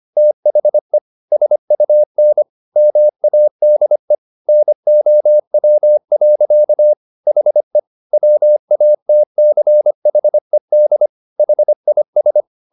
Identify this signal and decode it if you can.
Morse code, heard as a keyed tone, about 25 wpm, 600 Hz. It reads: THE SUN MADE NOW. HE WATCHED HIS
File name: the old man and the sea part 1 25wpm